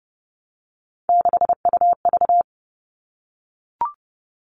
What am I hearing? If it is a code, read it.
Morse code, heard as a keyed tone, about 30 wpm, 700 Hz. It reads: 6UV
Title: All Cut Numbers - 3-Digit 30wpm